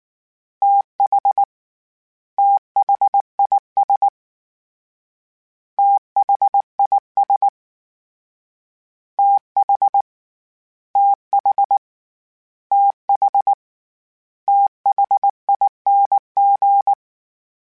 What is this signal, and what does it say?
Morse code, heard as a keyed tone, about 19 wpm, 800 Hz. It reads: TH THIS THIS TH TH TH THING
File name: THs / TH / TH Prefixes 3 @18